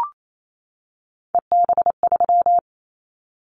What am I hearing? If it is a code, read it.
Morse code, heard as a keyed tone, about 28 wpm, 700 Hz. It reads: EB3
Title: Callsign Prefixes for Top 25 DXCC - 28wpm